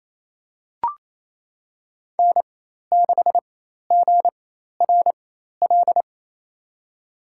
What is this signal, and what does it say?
Morse code, heard as a keyed tone, about 28 wpm, spaced as for 13 wpm, 700 Hz. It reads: N6GRL
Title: USA Call Signs - 4x Character Spacing - 28wpm